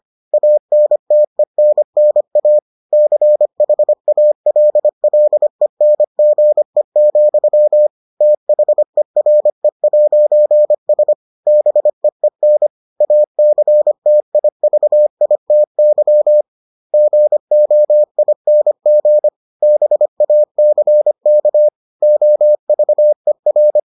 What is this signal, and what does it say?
Morse code, heard as a keyed tone, about 25 wpm, 600 Hz. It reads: ANTENNA CHALLENGE, THERE'S BEEN ACTIVITY GOING BACK OVER